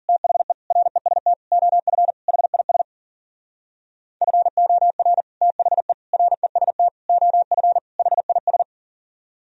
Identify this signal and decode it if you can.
Morse code, heard as a keyed tone, about 52 wpm, 700 Hz. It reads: THE REST OF HIS FOR THE REST OF HIS